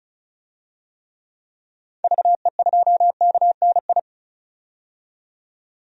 Morse code, heard as a keyed tone, about 36 wpm, 700 Hz. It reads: VE2KNI
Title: DXCC-100 Call Signs - 6 Characters - 35wpm